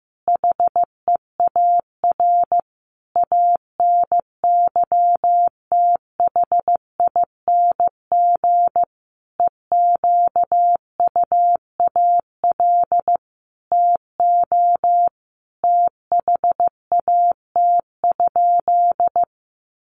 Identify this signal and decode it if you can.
Morse code, heard as a keyed tone, about 15 wpm, 700 Hz. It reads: HEAR ANYTHING EQUAL TO THAT?
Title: Sentences from Top 1400 Words 15wpm